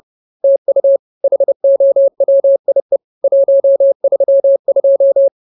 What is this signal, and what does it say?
Morse code, heard as a keyed tone, about 30 wpm, 550 Hz. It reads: TU HOWIE 132
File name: CWT-Chapter-03